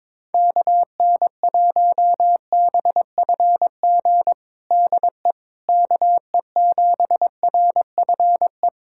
Morse code, heard as a keyed tone, about 22 wpm, 700 Hz. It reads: KN1BFG DE KE7RFE
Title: USA Callsign DE Callsign - 6 Characters - 22wpm